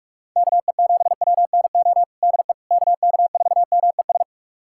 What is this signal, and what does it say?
Morse code, heard as a keyed tone, about 41 wpm, 700 Hz. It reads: KE7WNO DE KK4MES
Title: USA Callsign DE Callsign - 6 Characters - 45wpm